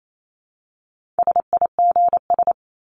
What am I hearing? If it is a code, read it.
Morse code, heard as a keyed tone, about 28 wpm, 700 Hz. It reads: SIGS
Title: Abbreviations 28wpm